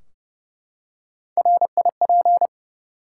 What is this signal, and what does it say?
Morse code, heard as a keyed tone, about 30 wpm, 700 Hz. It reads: RIP
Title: Three Letter Words - 30wpm